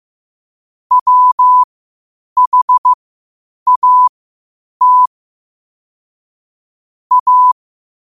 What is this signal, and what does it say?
Morse code, heard as a keyed tone, about 15 wpm, spaced as for 8 wpm, 1000 Hz. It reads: WHAT A
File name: qotdfiles / 8wpm